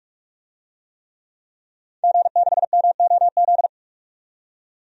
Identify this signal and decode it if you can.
Morse code, heard as a keyed tone, about 46 wpm, 700 Hz. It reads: G6MOZ
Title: DXCC-100 Call Signs - 5 Characters - 45wpm